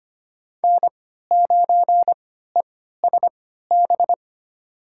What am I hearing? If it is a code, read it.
Morse code, heard as a keyed tone, about 25 wpm, spaced as for 14 wpm, 700 Hz. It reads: N9ESB
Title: USA Call Signs - 5 Characters - 3x Character Spacing - 25wpm